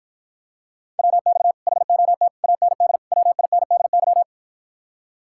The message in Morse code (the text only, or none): WX HOT AND WINDY